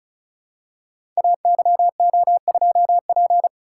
Morse code, heard as a keyed tone, about 36 wpm, 700 Hz. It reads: AYO2P
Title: Code Groups - 5 Letter:Numbers - 35wpm